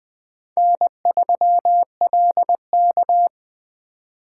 Morse code, heard as a keyed tone, about 20 wpm, 700 Hz. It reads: N3LK